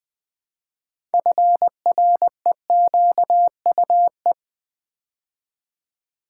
Morse code, heard as a keyed tone, about 20 wpm, 700 Hz. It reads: FREQUE
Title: Build A Word - 2 Letters At A Time - 8-Letter Words - 20wpm